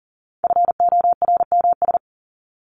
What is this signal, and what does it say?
Morse code, heard as a keyed tone, about 40 wpm, 700 Hz. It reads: FORMS
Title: ES:S Suffix - Top 100 Words with Base Words 40wpm